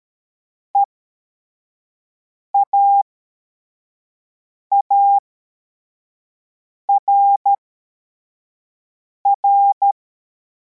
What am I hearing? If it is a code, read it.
Morse code, heard as a keyed tone, about 13 wpm, spaced as for 4 wpm, 800 Hz. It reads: EAARR